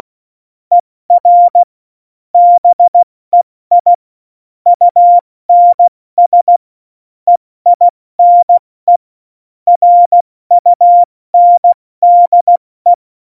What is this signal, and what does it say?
Morse code, heard as a keyed tone, about 16 wpm, 700 Hz. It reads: ER BEI UNS EINE RUNDE
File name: Morsekurs / Witze / Witz 24. November 2020, 70 F80